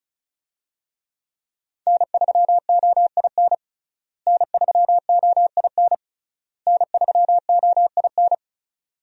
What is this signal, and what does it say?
Morse code, heard as a keyed tone, about 36 wpm, 700 Hz. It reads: N3OIN N3OIN N3OIN
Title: USA Call Signs - 5 Characters - Repeated 3x - 35wpm